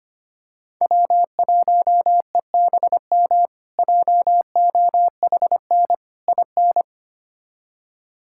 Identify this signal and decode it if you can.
Morse code, heard as a keyed tone, about 25 wpm, 700 Hz. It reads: W1EBM JOHN IN
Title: K1USN SST QSO Practice - 25wpm